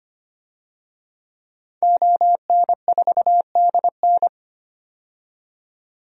Morse code, heard as a keyed tone, about 25 wpm, 700 Hz. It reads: ON4DN
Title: DXCC-100 Call Signs - 5 Characters - 25wpm